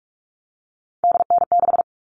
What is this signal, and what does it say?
Morse code, heard as a keyed tone, about 44 wpm, 700 Hz. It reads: DN6
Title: All Cut Numbers - 3-Digit 45wpm